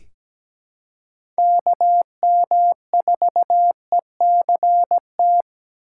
Morse code, heard as a keyed tone, about 17 wpm, 700 Hz. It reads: KM4ECT
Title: USA Call Signs - 8x Character Spacing - 17wpm